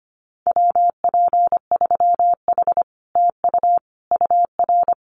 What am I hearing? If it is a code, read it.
Morse code, heard as a keyed tone, about 25 wpm, 700 Hz. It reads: WP3H TU UR
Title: Common QSO Phrases Encore 25wpm